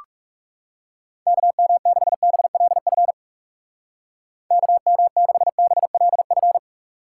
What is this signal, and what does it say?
Morse code, heard as a keyed tone, about 36 wpm, 700 Hz. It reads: KM6BLF KM6BLF
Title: Speed Racing - USA 2x3 Call Signs 30wpm